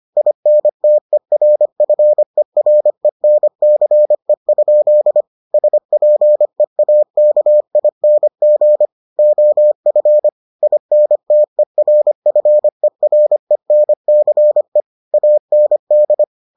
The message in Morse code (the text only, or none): INTERFERENCE? SPEAKING OF INTERFERENCE AND